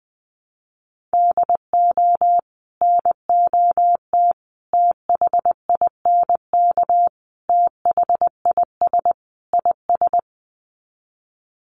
Morse code, heard as a keyed tone, about 19 wpm, 700 Hz. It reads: DO NOT THINK THIS IS
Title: Sets of 5 Words Encore2 20wpm